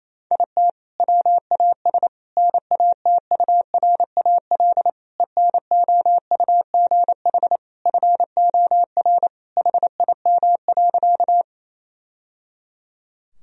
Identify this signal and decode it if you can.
Morse code, heard as a keyed tone, about 28 wpm, 700 Hz. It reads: IT WAS NATURAL ENOUGH FOR HIM.
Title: Sentences from Top 1000 Words - Review Part 4 - 28wpm